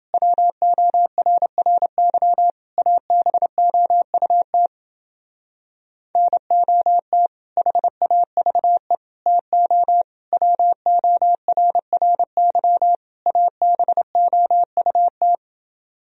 Morse code, heard as a keyed tone, about 29 wpm, 700 Hz. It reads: WORRY ABOUT NOT HAVE TO WORRY ABOUT